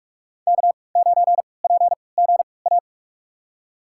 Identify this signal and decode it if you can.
Morse code, heard as a keyed tone, about 46 wpm, spaced as for 25 wpm, 700 Hz. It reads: K9PGA
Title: USA Call Signs - 5 Characters - 3x Character Spacing - 45wpm